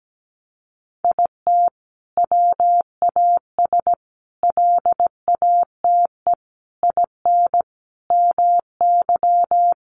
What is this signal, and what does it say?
Morse code, heard as a keyed tone, about 17 wpm, 700 Hz. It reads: IT WAS LATE IN MY
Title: Sentences from Top 1300 Words 17wpm